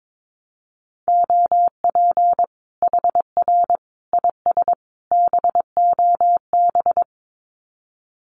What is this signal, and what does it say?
Morse code, heard as a keyed tone, about 22 wpm, 700 Hz. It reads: OP HR IS BOB
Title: Common QSO Phrases 22wpm